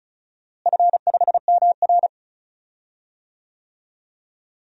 Morse code, heard as a keyed tone, about 34 wpm, 700 Hz. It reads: F5MR
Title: DXCC-100 Call Signs - 4 Characters - Repeated 3x - 35wpm